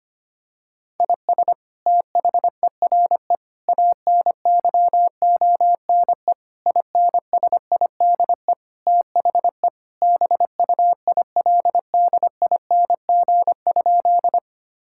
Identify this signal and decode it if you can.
Morse code, heard as a keyed tone, about 25 wpm, 700 Hz. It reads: IS THERE ANYONE INSIDE THE BUILDING?
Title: Sentences from Top 1000 Words - Review Part 6 - 25wpm